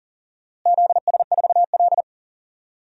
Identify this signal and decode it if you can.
Morse code, heard as a keyed tone, about 40 wpm, 700 Hz. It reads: ZS4L
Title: DXCC-100 Call Signs - 4 Characters - Repeated 2x - 40wpm